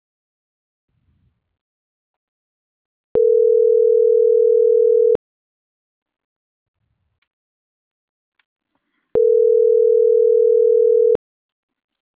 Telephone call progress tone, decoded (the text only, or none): ringback tone